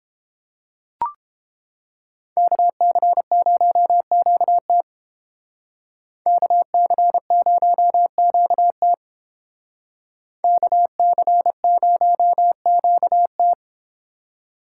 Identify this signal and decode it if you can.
Morse code, heard as a keyed tone, about 29 wpm, 700 Hz. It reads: KC0QT KC0QT KC0QT